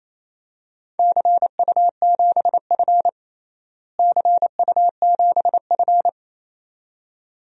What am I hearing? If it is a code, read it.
Morse code, heard as a keyed tone, about 28 wpm, 700 Hz. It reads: CU7F CU7F
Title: DXCC-100 Call Signs - 4 Characters - Repeated 2x - 28wpm